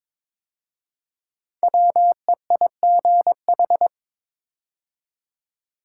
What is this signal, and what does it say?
Morse code, heard as a keyed tone, about 22 wpm, 700 Hz. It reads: WEIGH
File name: 5 Letter Words - First 3 Letters Given - 22wpm